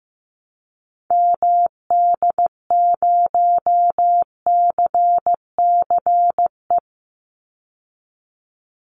Morse code, heard as a keyed tone, about 15 wpm, 700 Hz. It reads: MD0CCE
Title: Call Signs Repeated 2x - 15wpm